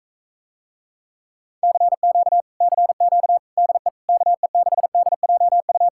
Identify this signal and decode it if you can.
Morse code, heard as a keyed tone, about 45 wpm, 700 Hz. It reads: CQ CQ DE KE6DJU